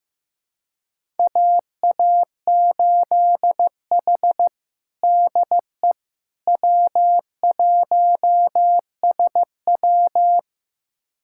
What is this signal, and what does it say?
Morse code, heard as a keyed tone, about 15 wpm, 700 Hz. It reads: AA8H DE W1SW
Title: USA Callsign DE Callsign - 4 Characters - 15wpm